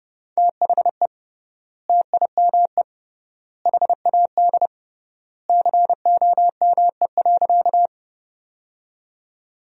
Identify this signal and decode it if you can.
Morse code, heard as a keyed tone, about 32 wpm, 700 Hz. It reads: THE TIME HAD COME.
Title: 3x Word-Spacing - Sentences from Top 100 Words - 30wpm